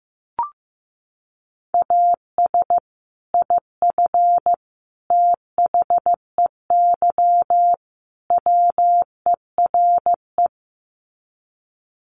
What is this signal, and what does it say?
Morse code, heard as a keyed tone, about 14 wpm, 700 Hz. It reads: AS IF THEY WERE
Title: Sets of 4 Words 15wpm